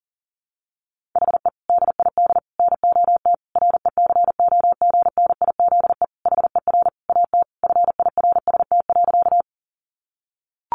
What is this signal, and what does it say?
Morse code, heard as a keyed tone, about 40 wpm, 700 Hz. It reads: HE DID NOT RECOGNIZE HER AT FIRST.
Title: Sentences from Top 1400 Words 40wpm